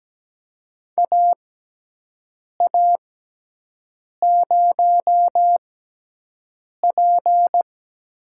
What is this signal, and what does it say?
Morse code, heard as a keyed tone, about 17 wpm, spaced as for 6 wpm, 700 Hz. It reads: AA0P
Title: USA Call Signs - 5 Characters - 6x Character Spacing - 17wpm